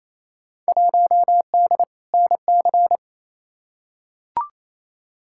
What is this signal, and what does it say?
Morse code, heard as a keyed tone, about 28 wpm, 700 Hz. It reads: 1D NC
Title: Field Day Exchanges Encore - 28wpm